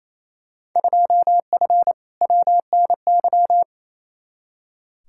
Morse code, heard as a keyed tone, about 28 wpm, 700 Hz. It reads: 2F WNY
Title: Field Day Exchanges Encore - 28wpm